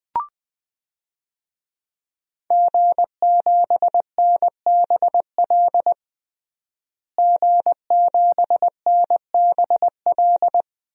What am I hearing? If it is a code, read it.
Morse code, heard as a keyed tone, about 20 wpm, 700 Hz. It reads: G7NBL G7NBL
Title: DXCC-100 Call Signs - 5 Characters - Repeated 2x - 20wpm